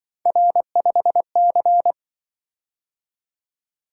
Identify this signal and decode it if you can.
Morse code, heard as a keyed tone, about 25 wpm, 700 Hz. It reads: R5C
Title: Speed-Racing - Three Letter-Number Combinations - 20wpm